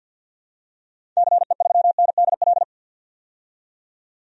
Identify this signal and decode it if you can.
Morse code, heard as a keyed tone, about 52 wpm, 700 Hz. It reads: CE3NDL
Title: DXCC-100 Call Signs - 6 Characters - 50wpm